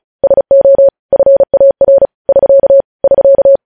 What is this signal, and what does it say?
Morse code, heard as a keyed tone, about 36 wpm, 550 Hz. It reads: SO FAR <SK> <SK>
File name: podcast / News headlines-1x-35x35